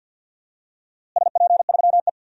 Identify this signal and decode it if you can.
Morse code, heard as a keyed tone, about 52 wpm, 700 Hz. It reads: SP3E
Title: DXCC-100 Call Signs - 4 Characters - Repeated 2x - 50wpm